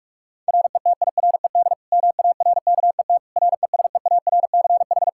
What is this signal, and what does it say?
Morse code, heard as a keyed tone, about 46 wpm, 700 Hz. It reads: RETIRED MARKET RESEARCH
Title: Practice QSO - Exchange 3 - More Info - 45wpm